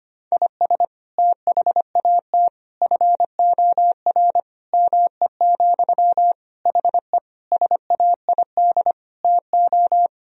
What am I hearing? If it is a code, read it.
Morse code, heard as a keyed tone, about 25 wpm, 700 Hz. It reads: IS THAT FOR ME, HE SAID TO
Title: Sentences from Top 1000 Words - Review Part 1 - 25wpm